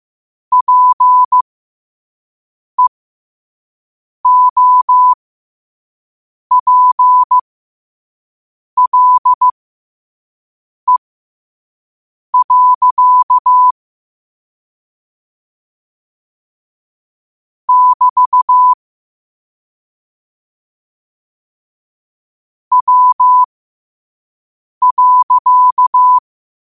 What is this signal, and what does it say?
Morse code, heard as a keyed tone, about 15 wpm, spaced as for 5 wpm, 1000 Hz. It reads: PEOPLE. <BT> W.
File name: qotdfiles / 5wpm